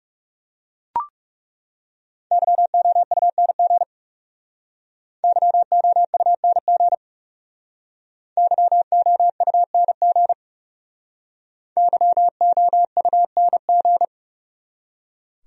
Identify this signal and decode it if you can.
Morse code, heard as a keyed tone, about 39 wpm, 700 Hz. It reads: YOUNG YOUNG YOUNG YOUNG